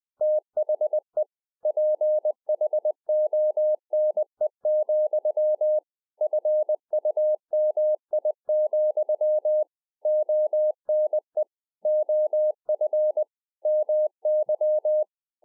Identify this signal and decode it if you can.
Morse code, heard as a keyed tone, about 20 wpm, 610 Hz. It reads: THE PHONE, FUMI, ONE OF MY